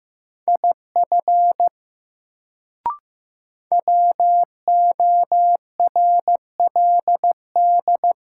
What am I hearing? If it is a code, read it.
Morse code, heard as a keyed tone, about 15 wpm, 700 Hz. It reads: IF WORLD
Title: Sets of 2 Words - New Word Drill - 15wpm